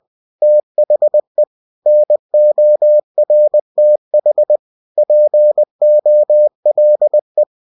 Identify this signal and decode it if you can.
Morse code, heard as a keyed tone, about 20 wpm, 600 Hz. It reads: THE NORTH POLE
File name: cw-books / winnie-the-pooh / winnie-20wpm-page-0084